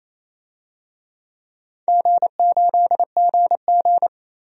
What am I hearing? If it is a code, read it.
Morse code, heard as a keyed tone, about 28 wpm, 700 Hz. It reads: G8GG
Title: DXCC-100 Call Signs - 4 Characters - 28wpm